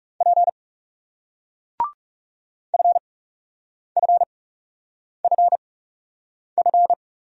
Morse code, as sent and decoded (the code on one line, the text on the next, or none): .--. / ..-. ..-. ..-. ..-.
P FFFF